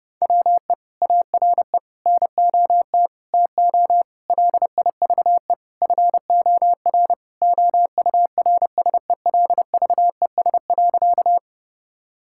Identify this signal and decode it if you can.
Morse code, heard as a keyed tone, about 32 wpm, 700 Hz. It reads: WE ARE NOT TO LIVE FOR OURSELVES.
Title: Sentences from Top 1600 Words 30wpm